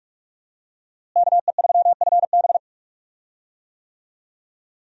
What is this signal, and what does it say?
Morse code, heard as a keyed tone, about 46 wpm, 700 Hz. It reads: KE3FB
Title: USA Call Signs - 5 Characters - Repeated 2x - 45wpm